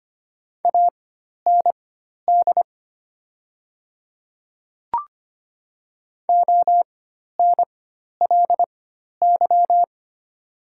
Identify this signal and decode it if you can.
Morse code, heard as a keyed tone, about 25 wpm, spaced as for 12 wpm, 700 Hz. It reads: AND ONLY
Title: 4x Character Spacing - Top 300 Words - 25wpm